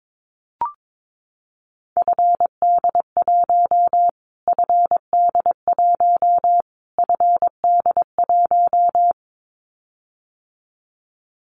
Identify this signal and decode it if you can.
Morse code, heard as a keyed tone, about 22 wpm, 700 Hz. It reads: FD1 FD1 FD1
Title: Callsign Prefixes for Top 25 DXCC - Repeated 3x - 22wpm